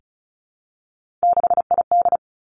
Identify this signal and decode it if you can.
Morse code, heard as a keyed tone, about 36 wpm, 700 Hz. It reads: 6ID
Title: Three Challenging Letter-Number Combinations 35wpm